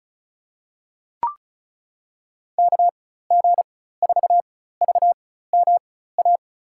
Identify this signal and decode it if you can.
Morse code, heard as a keyed tone, about 36 wpm, spaced as for 16 wpm, 700 Hz. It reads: KG4VMA
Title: USA Call Signs - 6 Characters - 4x Character Spacing - 35wpm